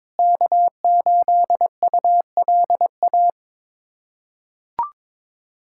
Morse code, heard as a keyed tone, about 22 wpm, 700 Hz. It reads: K8ULA
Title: USA Callsign DE Callsign 22wpm